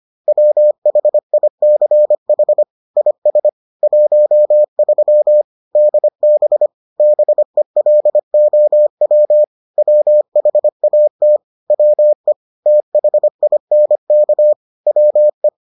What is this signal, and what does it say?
Morse code, heard as a keyed tone, about 25 wpm, 600 Hz. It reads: WHICH IS 13 DB BELOW WHAT WE THINK WE